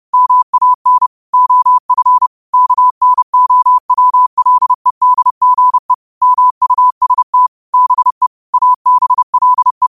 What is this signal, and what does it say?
Morse code, heard as a keyed tone, about 30 wpm, 1000 Hz. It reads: MAN OF KNOWLEDGE MUST BE ABLE
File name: qotdfiles / 30wpm